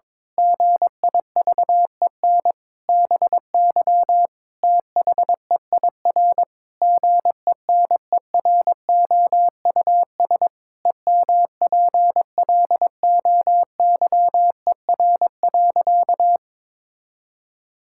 Morse code, heard as a keyed tone, about 22 wpm, 700 Hz. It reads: GIVEN BY THEIR GENEROUS EMPLOYER.
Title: Pangram Challenge v2 - 22wpm